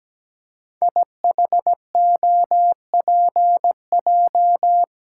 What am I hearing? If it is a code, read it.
Morse code, heard as a keyed tone, about 17 wpm, 700 Hz. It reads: IHOPJ
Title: Code Groups - 5 Letters - 17wpm